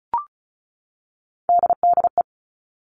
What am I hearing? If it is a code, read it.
Morse code, heard as a keyed tone, about 34 wpm, 700 Hz. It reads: DDE
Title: All Cut Numbers - 3-Digit 35wpm